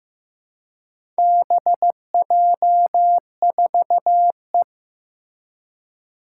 Morse code, heard as a keyed tone, about 15 wpm, 700 Hz. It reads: BJ4E
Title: Four Challenging Letter-Number Combinations 15wpm